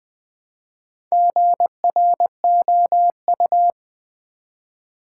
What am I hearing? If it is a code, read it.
Morse code, heard as a keyed tone, about 20 wpm, 700 Hz. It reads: GROU
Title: Build A Word - 1 Letter At A Time - 5-Letter Words - 20wpm